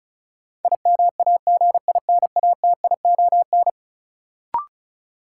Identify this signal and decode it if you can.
Morse code, heard as a keyed tone, about 36 wpm, 700 Hz. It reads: IMAGINATION
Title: Sentences from Top 1300 Words - New Word Drill - 35wpm